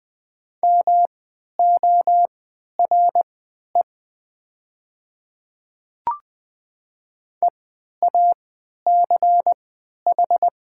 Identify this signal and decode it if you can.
Morse code, heard as a keyed tone, about 20 wpm, spaced as for 11 wpm, 700 Hz. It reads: MORE EACH
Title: 3x Character Spacing - Top 100 Words - 20wpm